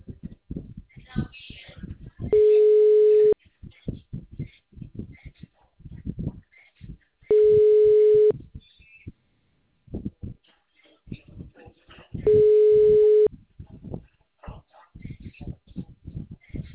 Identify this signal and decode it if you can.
Ringback tone (Europe, 425 Hz).